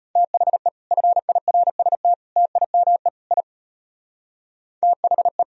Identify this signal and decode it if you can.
Morse code, heard as a keyed tone, about 33 wpm, 700 Hz. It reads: THE FIRST TIME I THE